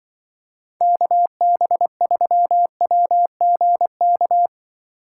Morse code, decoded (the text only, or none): KB3WGK